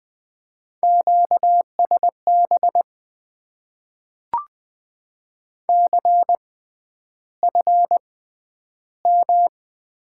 Morse code, heard as a keyed tone, about 20 wpm, spaced as for 10 wpm, 700 Hz. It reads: QSB CFM
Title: Top 90 QSO Elements - 6x Character Spacing - 20wpm